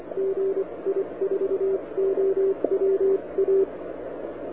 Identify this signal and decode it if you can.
Morse code, heard as a keyed tone, about 25 wpm, spaced as for 21 wpm, 390 Hz. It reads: GI4OWA
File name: wwcw08 / k5zd wwcw08 1700z